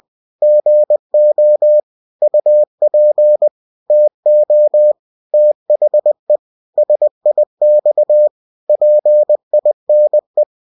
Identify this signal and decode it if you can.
Morse code, heard as a keyed tone, about 20 wpm, 600 Hz. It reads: GO UP TO THE SIX PINE
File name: cw-books / winnie-the-pooh / winnie-20wpm-page-0041